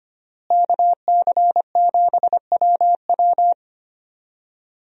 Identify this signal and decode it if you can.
Morse code, heard as a keyed tone, about 25 wpm, 700 Hz. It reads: KC7WW
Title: USA Call Signs - 5 Characters - Repeated 3x - 25wpm